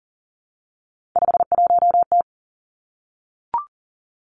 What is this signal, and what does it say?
Morse code, heard as a keyed tone, about 40 wpm, 700 Hz. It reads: 51T
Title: Three Challenging Letter-Number Combinations 40wpm